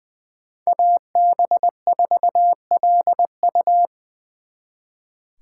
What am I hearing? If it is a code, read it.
Morse code, heard as a keyed tone, about 24 wpm, 700 Hz. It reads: AB4LU
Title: Speed Racing - USA 2x2 Call Signs 20wpm